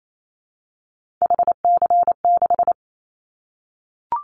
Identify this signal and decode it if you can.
Morse code, heard as a keyed tone, about 28 wpm, 700 Hz. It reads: HC6
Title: Callsign Prefixes for Top 100 DXCC - 28wpm